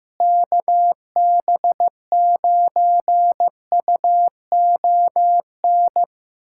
Morse code, heard as a keyed tone, about 15 wpm, 700 Hz. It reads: KB9UON
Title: USA Callsign DE Callsign - 6 Characters - 15wpm